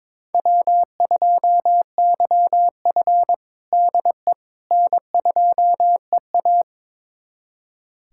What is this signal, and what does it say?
Morse code, heard as a keyed tone, about 22 wpm, 700 Hz. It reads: W2YF DE N2EA
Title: USA Callsign DE Callsign - 4 Characters - 22wpm